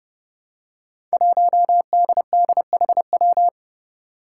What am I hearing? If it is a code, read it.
Morse code, heard as a keyed tone, about 30 wpm, 700 Hz. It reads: 1DDHW
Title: Five Challenging Letter-Number Combinations 30wpm